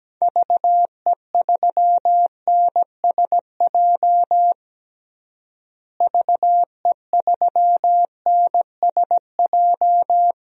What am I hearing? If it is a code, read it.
Morse code, heard as a keyed tone, about 17 wpm, 700 Hz. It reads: VE3NSJ VE3NSJ
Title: DXCC-100 Call Signs - 6 Characters - Repeated 3x - 17wpm